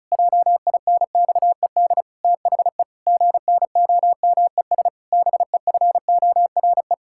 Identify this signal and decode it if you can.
Morse code, heard as a keyed tone, about 36 wpm, 700 Hz. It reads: JINXED THE GNOMES BEFORE
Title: Pangram Challenge v2 - 35wpm